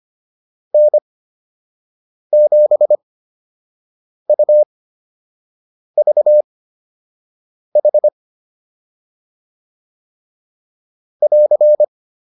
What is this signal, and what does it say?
Morse code, heard as a keyed tone, about 25 wpm, spaced as for 6 wpm, 600 Hz. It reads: N7UVH <AR>
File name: QSO 105 06